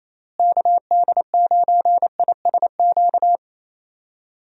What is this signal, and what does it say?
Morse code, heard as a keyed tone, about 28 wpm, 700 Hz. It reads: KD9ISQ
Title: USA Call Signs - 6 Characters - Repeated 3x - 28wpm